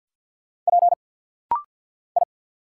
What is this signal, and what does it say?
Morse code, heard as a keyed tone, about 50 wpm, 700 Hz. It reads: P I
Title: Single Letter-Number Alt - Rapid-Fire 50wpm